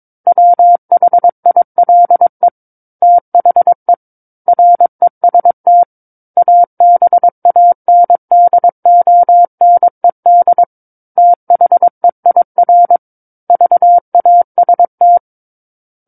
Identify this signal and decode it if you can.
Morse code, heard as a keyed tone, about 22 wpm, 700 Hz. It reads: WHILE THE REST ABANDONED THEIR VAST
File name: Book One / 13-Chapter 13 - 22 WPM